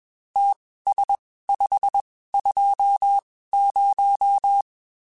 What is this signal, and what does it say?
Morse code, heard as a keyed tone, about 21 wpm, spaced as for 15 wpm, 800 Hz. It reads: TS520